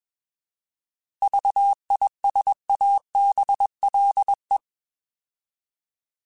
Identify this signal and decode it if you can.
Morse code, heard as a keyed tone, about 21 wpm, 800 Hz. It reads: VISABLE
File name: Alphabetical / Alphabetical mp3s / Vs @20